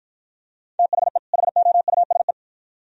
Morse code, heard as a keyed tone, about 53 wpm, 700 Hz. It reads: THE HOUSE